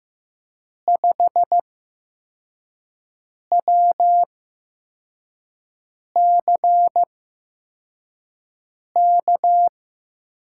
Morse code, heard as a keyed tone, about 15 wpm, spaced as for 4 wpm, 700 Hz. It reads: 5WCK
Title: USA Call Signs - 8x Character Spacing - 15wpm